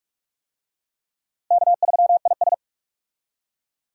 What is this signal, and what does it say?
Morse code, heard as a keyed tone, about 47 wpm, 700 Hz. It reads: K3IS
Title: USA Call Signs - 4 Characters - Repeated 2x - 45wpm